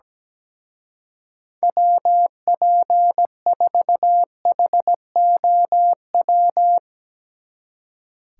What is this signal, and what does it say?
Morse code, heard as a keyed tone, about 19 wpm, 700 Hz. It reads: WP4HOW
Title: Speed Racing - USA 2x3 Call Signs 17wpm